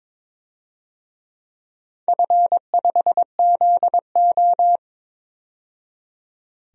Morse code, heard as a keyed tone, about 22 wpm, 700 Hz. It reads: F5ZO
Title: DXCC-100 Call Signs - 4 Characters - 22wpm